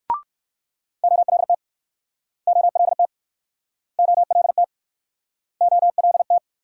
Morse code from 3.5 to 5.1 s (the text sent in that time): OLT